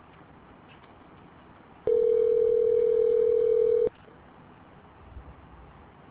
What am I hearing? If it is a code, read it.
Ringback tone (North America), 440 + 480 Hz.